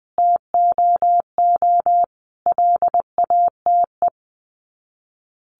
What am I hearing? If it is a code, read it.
Morse code, heard as a keyed tone, about 20 wpm, 700 Hz. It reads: TOO LATE
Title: Sets of 5 Words Encore2 20wpm